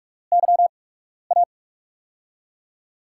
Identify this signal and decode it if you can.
Morse code, heard as a keyed tone, about 45 wpm, spaced as for 12 wpm, 700 Hz. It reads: YA
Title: USA Call Signs - 6 Characters - 8x Character Spacing - 45wpm